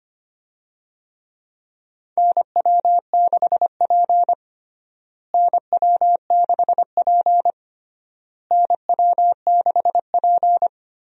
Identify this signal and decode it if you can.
Morse code, heard as a keyed tone, about 25 wpm, 700 Hz. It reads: NW6P NW6P NW6P
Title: USA Call Signs - 4 Characters - Repeated 3x - 25wpm